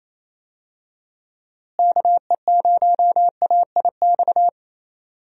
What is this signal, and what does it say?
Morse code, heard as a keyed tone, about 28 wpm, 700 Hz. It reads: KE0AIX
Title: USA Call Signs - 6 Characters - Repeated 2x - 28wpm